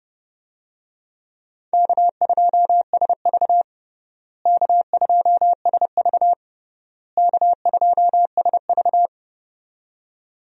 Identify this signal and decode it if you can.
Morse code, heard as a keyed tone, about 30 wpm, 700 Hz. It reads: K2SV K2SV K2SV
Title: USA Call Signs - 4 Characters - Repeated 3x - 30wpm